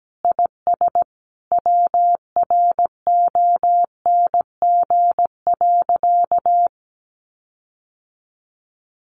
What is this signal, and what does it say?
Morse code, heard as a keyed tone, about 17 wpm, 700 Hz. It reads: IS WRONG.
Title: Sentences from Top 1400 Words 17wpm